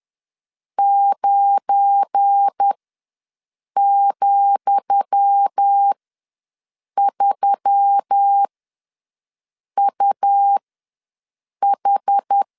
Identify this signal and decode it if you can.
Morse code, heard as a keyed tone, about 10 wpm, spaced as for 6 wpm, 800 Hz. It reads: 9,3UH
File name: LEZIONI / 10 SIGNS / 10D COMA ws 10wpm 4314